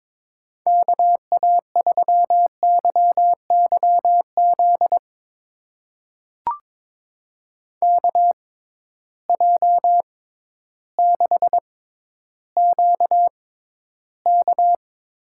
Morse code, heard as a keyed tone, about 22 wpm, spaced as for 12 wpm, 700 Hz. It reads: KA3YYZ KJ6QK
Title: USA Call Signs - 6 Characters - 6x Character Spacing - 22wpm